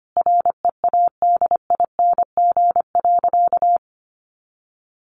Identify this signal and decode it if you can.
Morse code, heard as a keyed tone, about 25 wpm, 700 Hz. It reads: READING.
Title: Sentences from Top 1000 Words 25wpm